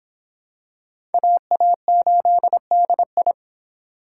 Morse code, heard as a keyed tone, about 24 wpm, 700 Hz. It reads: AA8DI